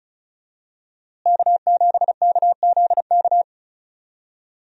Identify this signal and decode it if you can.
Morse code, heard as a keyed tone, about 34 wpm, 700 Hz. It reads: K7KZK
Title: USA Call Signs - 5 Characters - Repeated 2x - 35wpm